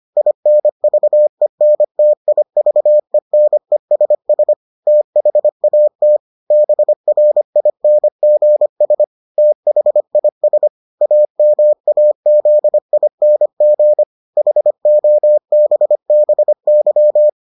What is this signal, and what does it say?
Morse code, heard as a keyed tone, about 25 wpm, 600 Hz. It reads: INVENTIVENESS THAT BRINGS THIS AMAZING HOBBY